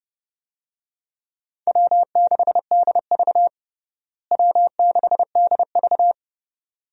Morse code, heard as a keyed tone, about 30 wpm, 700 Hz. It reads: W6DV W6DV
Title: USA Call Signs - 4 Characters - Repeated 2x - 30wpm